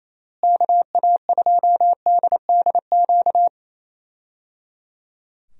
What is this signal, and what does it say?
Morse code, heard as a keyed tone, about 28 wpm, 700 Hz. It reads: KA2DDQ
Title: USA Call Signs - 6 Characters - Repeated 3x - 28wpm